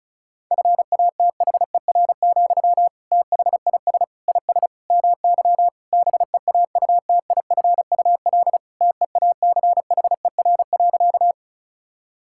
Morse code, heard as a keyed tone, about 34 wpm, 700 Hz. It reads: FATHER, THIS IS MY BEAUTIFUL TEACHER.
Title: Sentences from Top 1000 Words - Review Part 6 - 35wpm